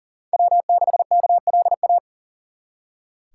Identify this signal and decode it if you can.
Morse code, heard as a keyed tone, about 52 wpm, 700 Hz. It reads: W6KLA